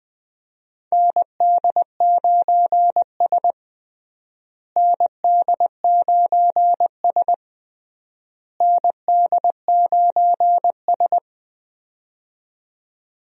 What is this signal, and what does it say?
Morse code, heard as a keyed tone, about 20 wpm, 700 Hz. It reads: ND9S ND9S ND9S
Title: USA Call Signs - 4 Characters - Repeated 3x - 20wpm